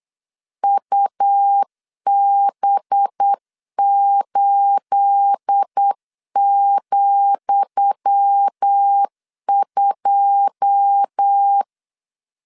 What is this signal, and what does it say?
Morse code, heard as a keyed tone, about 8 wpm, 800 Hz. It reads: UB8,2
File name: LEZIONI / 10 SIGNS / 10D COMA 8wpm 3749